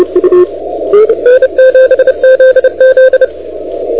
Morse code, heard as a keyed tone, about 30 wpm, 1550 Hz, with radio noise. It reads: I7ZZ
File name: recordings / AE5GT de NT1K ARRL10M CW 28MHz